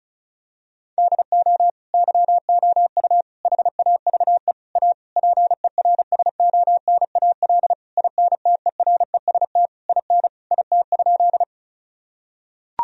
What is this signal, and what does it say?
Morse code, heard as a keyed tone, about 34 wpm, 700 Hz. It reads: DO YOU HAVE A PERSONAL INTEREST IN IT?
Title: Sentences from Top 1000 Words - Review Part 4 - 35wpm